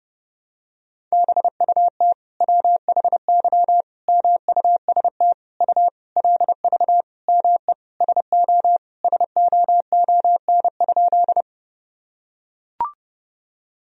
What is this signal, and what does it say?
Morse code, heard as a keyed tone, about 30 wpm, 700 Hz. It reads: BUT WHY MUST U LV ME SO SOON?
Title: Sentences from Top 400 Words and Abbreviations 30wpm